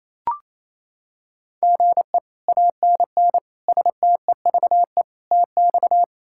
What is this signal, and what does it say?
Morse code, heard as a keyed tone, about 27 wpm, 700 Hz. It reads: GE ANN STEVE TX
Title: K1USN SST QSO Practice - 28wpm